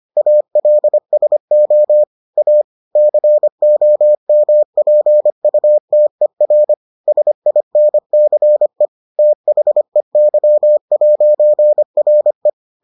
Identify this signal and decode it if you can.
Morse code, heard as a keyed tone, about 25 wpm, 600 Hz. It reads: ALSO A COMPUTER SINCE THEY'RE